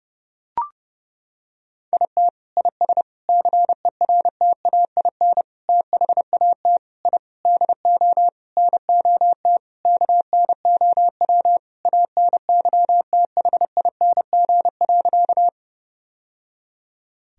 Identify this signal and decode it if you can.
Morse code, heard as a keyed tone, about 30 wpm, 700 Hz. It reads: IT IS CERTAIN THAT I DO NOT KNOW ANYTHING.
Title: Sentences from Top 1000 Words - Review Part 2 - 30wpm